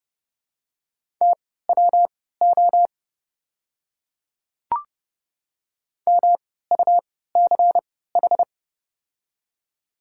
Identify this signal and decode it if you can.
Morse code, heard as a keyed tone, about 30 wpm, spaced as for 17 wpm, 700 Hz. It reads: TWO MUCH
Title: 3x Character Spacing - Top 200 Words - 30wpm